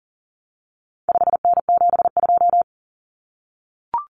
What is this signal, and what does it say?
Morse code, heard as a keyed tone, about 40 wpm, 700 Hz. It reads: 5N72
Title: Common Cut Numbers - 4-Digit 40wpm